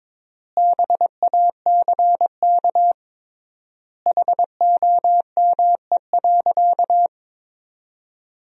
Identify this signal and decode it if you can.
Morse code, heard as a keyed tone, about 22 wpm, 700 Hz. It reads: BACK HOME.
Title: 3x Word-Spacing - Sentences from Top 500 Words - 22wpm